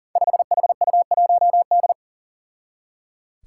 Morse code, heard as a keyed tone, about 39 wpm, 700 Hz. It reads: 5HU1D
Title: Five Challenging Letter-Number Combinations 40wpm